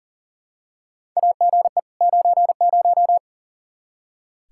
Morse code, heard as a keyed tone, about 40 wpm, 700 Hz. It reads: AGE 90
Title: Practice QSO - Exchange 3 - More Info - 40wpm